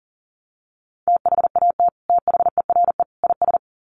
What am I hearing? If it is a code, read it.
Morse code, heard as a keyed tone, about 40 wpm, 700 Hz. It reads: THAT THERE IS
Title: Sets of 3 Words 40wpm